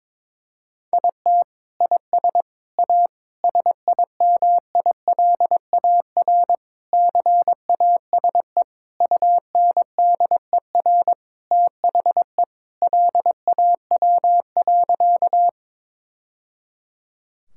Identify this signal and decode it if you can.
Morse code, heard as a keyed tone, about 22 wpm, 700 Hz. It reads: IT IS A SIMILAR CASE UNDER THE LAW.
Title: Sentences from Top 1000 Words - Review Part 5 - 22wpm